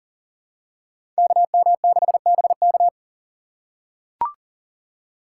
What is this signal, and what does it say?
Morse code, heard as a keyed tone, about 40 wpm, 690 Hz. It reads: KM6BK
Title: USA Call Signs - 3x Character Spacing - 40wpm